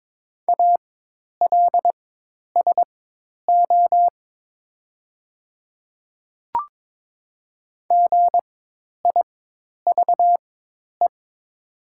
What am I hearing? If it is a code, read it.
Morse code, heard as a keyed tone, about 22 wpm, spaced as for 10 wpm, 700 Hz. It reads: ALSO GIVE
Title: 4x Character Spacing - Top 300 Words - 22wpm